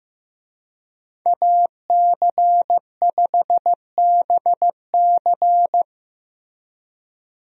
Morse code, heard as a keyed tone, about 15 wpm, 700 Hz. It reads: AC5BC
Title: USA Call Signs - 5 Characters - Repeated 2x - 15wpm